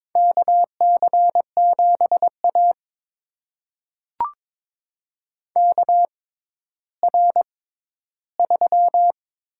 Morse code, heard as a keyed tone, about 22 wpm, spaced as for 12 wpm, 700 Hz. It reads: KC7A KR3
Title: USA Call Signs - 4 Characters - 6x Character Spacing - 22wpm